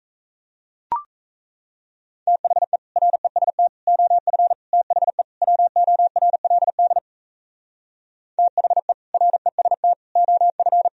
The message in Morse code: THE REST OF THE WORLD THE REST OF

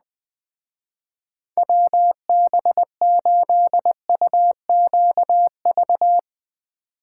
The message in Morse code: WB8UQV

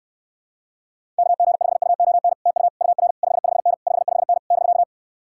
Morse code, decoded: KG5UON TU UR 55N 55N <BK>

